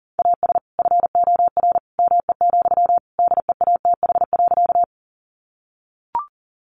Morse code, heard as a keyed tone, about 40 wpm, 700 Hz. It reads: AS FOR ME, DEATH.
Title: Sentences from Top 1100 Words 40wpm